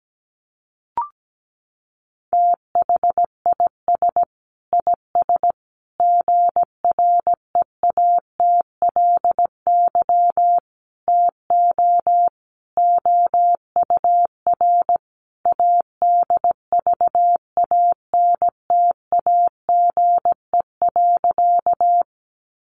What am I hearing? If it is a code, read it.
Morse code, heard as a keyed tone, about 17 wpm, 700 Hz. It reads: THIS IS GREATLY TO OUR ADVANTAGE.
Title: Sentences from Top 1400 Words 17wpm